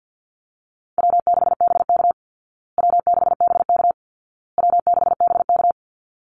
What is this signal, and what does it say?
Morse code, heard as a keyed tone, about 50 wpm, 700 Hz. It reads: W6BK W6BK W6BK
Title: Easy Call Signs Repeated 3x - 50wpm